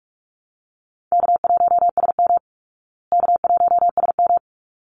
Morse code, heard as a keyed tone, about 44 wpm, 700 Hz. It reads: K1SM K1SM
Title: Easy Call Signs Repeated 2x - 45wpm